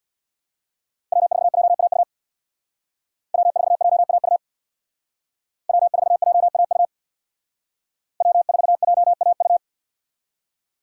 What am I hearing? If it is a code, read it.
Morse code, heard as a keyed tone, about 64 wpm, 700 Hz. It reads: W4JAU W4JAU W4JAU W4JAU